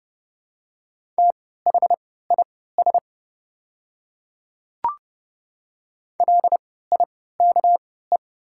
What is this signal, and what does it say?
Morse code, heard as a keyed tone, about 30 wpm, spaced as for 17 wpm, 700 Hz. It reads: THIS LIKE
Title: 3x Character Spacing - Top 100 Words - 30wpm